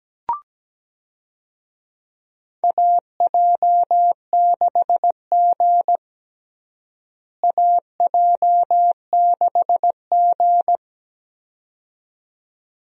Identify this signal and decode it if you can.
Morse code, heard as a keyed tone, about 17 wpm, 700 Hz. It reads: AJ6G AJ6G
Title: USA Call Signs - 4 Characters - Repeated 2x - 17wpm